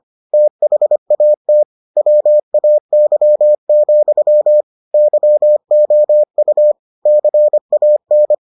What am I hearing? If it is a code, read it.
Morse code, heard as a keyed tone, about 25 wpm, 600 Hz. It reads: THAT WAY, YOU CAN